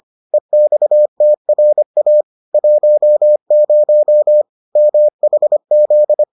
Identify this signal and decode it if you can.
Morse code, heard as a keyed tone, about 25 wpm, 600 Hz. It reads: EXTRA 10 MHZ